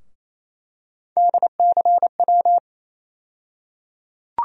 Morse code, heard as a keyed tone, about 28 wpm, 700 Hz. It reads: DCW
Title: USA Call Sign Postfixes - 3 Letters 28wpm